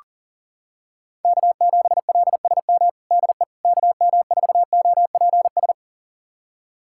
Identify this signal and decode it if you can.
Morse code, heard as a keyed tone, about 40 wpm, 700 Hz. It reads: K7LSM DE KM4OPS
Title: USA Callsign DE Callsign 40wpm